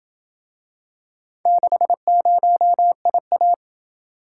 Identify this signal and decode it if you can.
Morse code, heard as a keyed tone, about 26 wpm, 700 Hz. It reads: 60IA